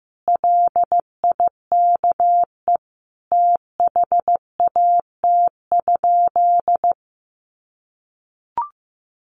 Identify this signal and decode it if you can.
Morse code, heard as a keyed tone, about 15 wpm, 700 Hz. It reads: LIKE THAT?
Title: Sentences from Top 1400 Words 15wpm